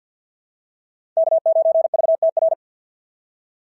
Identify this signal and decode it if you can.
Morse code, heard as a keyed tone, about 50 wpm, 690 Hz. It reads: K9VTR
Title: SOTA QSO Practice 50wpm